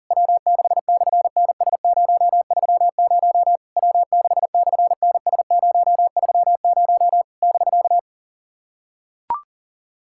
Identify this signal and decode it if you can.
Morse code, heard as a keyed tone, about 40 wpm, 700 Hz. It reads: W6/NS030 W6/NS030 <BK>
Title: SOTA QSO Practice 40wpm